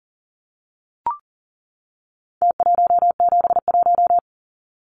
Morse code, heard as a keyed tone, about 40 wpm, 700 Hz. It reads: T171
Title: Numbers - Contest Style - 4-Digit 40wpm